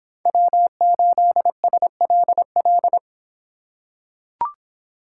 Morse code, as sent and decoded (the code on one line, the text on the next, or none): .-- ---.. ... .-.. .-..
W8SLL